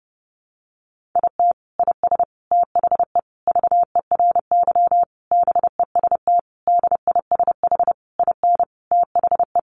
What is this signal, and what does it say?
Morse code, heard as a keyed tone, about 30 wpm, 700 Hz. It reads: IT IS THE VERY BEST DISH IN THE
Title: Sentences from Top 1300 Words 30wpm